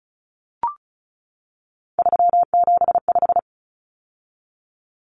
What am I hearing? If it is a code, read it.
Morse code, heard as a keyed tone, about 34 wpm, 700 Hz. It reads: 375
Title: Common Cut Numbers - 3-Digit 35wpm